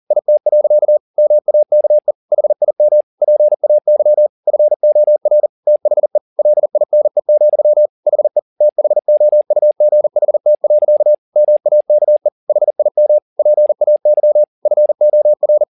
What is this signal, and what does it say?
Morse code, heard as a keyed tone, about 40 wpm, 600 Hz. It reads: IT. MAKE HIM PAY FOR THE LINE, HE THOUGHT. MAKE HIM PAY FOR
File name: the old man and the sea 2 40wpm